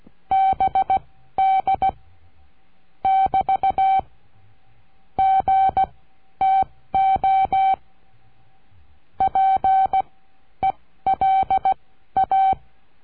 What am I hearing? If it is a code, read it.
Morse code, heard as a keyed tone, about 16 wpm, spaced as for 12 wpm, 770 Hz. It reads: BD <BT> GTO PELA
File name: cursos / CURSO DE TELEGRAFIA / Telegrafia - Trilha 22